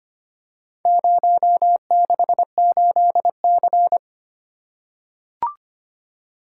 Morse code, heard as a keyed tone, about 25 wpm, 700 Hz. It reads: 068C
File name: practice / 25wpm / Four Letter-Number Combinations - 25wpm